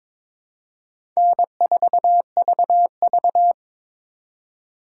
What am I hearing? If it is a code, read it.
Morse code, heard as a keyed tone, about 22 wpm, 700 Hz. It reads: N4VV
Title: USA Call Signs - 4x Character Spacing - 22wpm